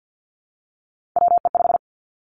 Morse code, heard as a keyed tone, about 50 wpm, 700 Hz. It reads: WE5
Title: Three Challenging Letter-Number Combinations 50wpm